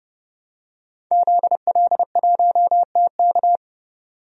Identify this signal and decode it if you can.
Morse code, heard as a keyed tone, about 30 wpm, 700 Hz. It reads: ZL1TK
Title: DXCC-100 Call Signs - 5 Characters - 30wpm